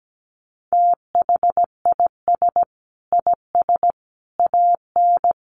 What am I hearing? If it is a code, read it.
Morse code, heard as a keyed tone, about 17 wpm, 700 Hz. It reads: THIS IS AN
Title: Sentences from Top 1300 Words 17wpm